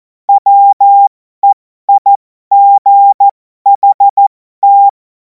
Morse code, heard as a keyed tone, about 14 wpm, spaced as for 12 wpm, 800 Hz. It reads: WEIGHT
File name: Morsekurs / Witze